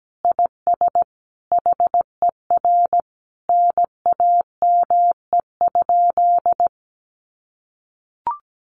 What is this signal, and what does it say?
Morse code, heard as a keyed tone, about 17 wpm, 700 Hz. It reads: IS HER NAME?
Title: Sentences from Top 200 Words 17wpm